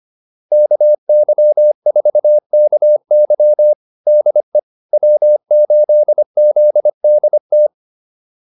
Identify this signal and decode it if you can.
Morse code, heard as a keyed tone, about 25 wpm, 600 Hz. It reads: KY4KY DE W8ZDT